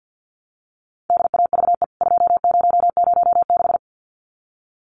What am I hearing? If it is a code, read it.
Morse code, heard as a keyed tone, about 49 wpm, 700 Hz. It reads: DAVE 2006
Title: CWT Exchange 50wpm